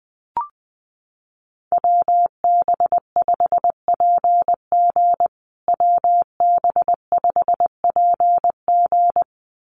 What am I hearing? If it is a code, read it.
Morse code, heard as a keyed tone, about 20 wpm, 700 Hz. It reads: WB5PG WB5PG
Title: USA Call Signs Repeated 2x 20wpm